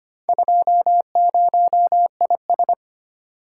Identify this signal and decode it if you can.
Morse code, heard as a keyed tone, about 25 wpm, 700 Hz. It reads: 20IS
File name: practice / 25wpm / Four Letter-Number Combinations - 25wpm